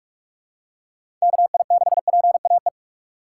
Morse code, heard as a keyed tone, about 42 wpm, 700 Hz. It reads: KI6PAE
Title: USA Call Signs - 6 Characters - 2x Character Spacing - 45wpm